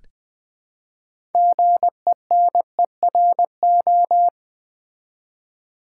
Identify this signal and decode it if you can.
Morse code, heard as a keyed tone, about 21 wpm, 700 Hz. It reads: GENERO
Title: Build A Word - 3 Letters At A Time - 8-Letter Words - 20wpm